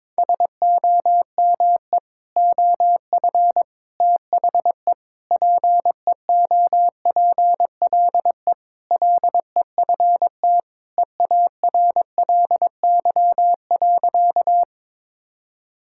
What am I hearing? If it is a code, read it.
Morse code, heard as a keyed tone, about 21 wpm, 700 Hz. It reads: SOME OF THE PEOPLE LEFT EARLY.
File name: practice / 22wpm / Sentences with 3-Word Phrases - 22wpm